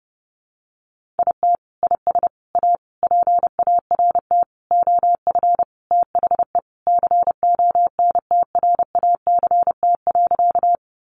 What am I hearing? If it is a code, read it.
Morse code, heard as a keyed tone, about 30 wpm, 700 Hz. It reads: IT IS A PART OF THE CONTRACT.
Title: Sentences from Top 1400 Words 30wpm